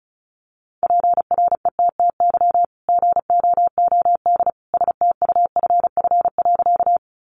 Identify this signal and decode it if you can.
Morse code, heard as a keyed tone, about 34 wpm, 700 Hz. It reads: PRETTY GOOD STUFF.
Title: Sentences from Top 1400 Words 35wpm